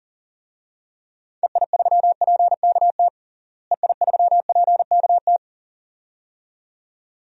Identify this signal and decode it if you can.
Morse code, heard as a keyed tone, about 40 wpm, 700 Hz. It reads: EI3PKT EI3PKT
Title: DXCC-100 Call Signs - 6 Characters - Repeated 2x - 40wpm